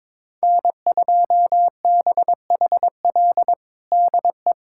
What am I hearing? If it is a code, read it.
Morse code, heard as a keyed tone, about 22 wpm, 700 Hz. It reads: N2BHL DE
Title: USA Callsign DE Callsign - 5 Characters - 22wpm